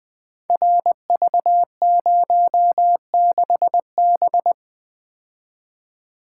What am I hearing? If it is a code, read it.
Morse code, heard as a keyed tone, about 20 wpm, 700 Hz. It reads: RV06B
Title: Code Groups - 5 Letter:Numbers - 20wpm